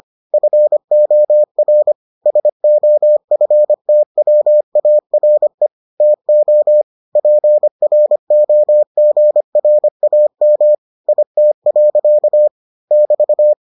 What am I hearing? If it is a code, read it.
Morse code, heard as a keyed tone, about 25 wpm, 600 Hz. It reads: FOR SOFTWARE TO PROGRAM IT. <BT>